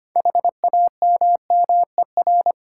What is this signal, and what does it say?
Morse code, heard as a keyed tone, about 25 wpm, 690 Hz. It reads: HAMMER
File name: practice / 25wpm / Easy Words with Hints - 25wpm